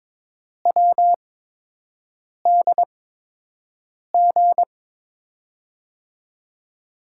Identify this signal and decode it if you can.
Morse code, heard as a keyed tone, about 22 wpm, spaced as for 6 wpm, 700 Hz. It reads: WDG
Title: USA Call Signs - 8x Character Spacing - 22wpm